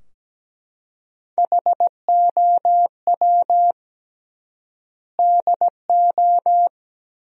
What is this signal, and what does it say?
Morse code, heard as a keyed tone, about 17 wpm, 700 Hz. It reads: HOW DO
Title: 3x Word-Spacing - Sentences from Top 400 Words - 17wpm